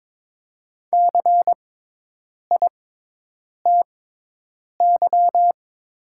Morse code, heard as a keyed tone, about 22 wpm, spaced as for 8 wpm, 700 Hz. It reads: CITY
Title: 6x Character Spacing - Top 300 Words - 22wpm